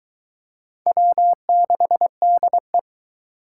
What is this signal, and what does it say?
Morse code, heard as a keyed tone, about 21 wpm, 700 Hz. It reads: W6DE